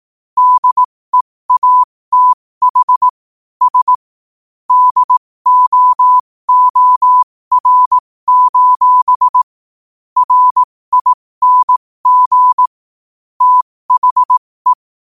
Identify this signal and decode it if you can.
Morse code, heard as a keyed tone, about 18 wpm, spaced as for 15 wpm, 1000 Hz. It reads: DEATH S DOOR: RING THE
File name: qotdfiles / 15wpm